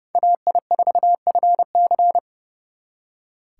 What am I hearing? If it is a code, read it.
Morse code, heard as a keyed tone, about 38 wpm, 700 Hz. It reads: AI4FC